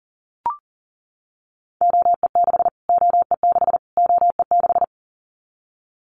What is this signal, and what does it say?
Morse code, heard as a keyed tone, about 40 wpm, 700 Hz. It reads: OE6 OE6 OE6
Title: Callsign Prefixes for Top 25 DXCC - Repeated 3x - 40wpm